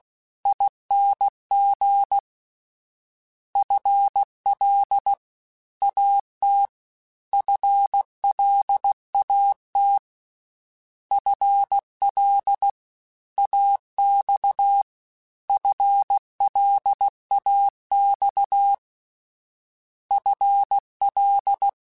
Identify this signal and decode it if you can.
Morse code, heard as a keyed tone, about 16 wpm, 800 Hz. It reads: ING FL AT FLAT FL AX FLAX FL
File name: FL / FL mp3s / Fast FL @15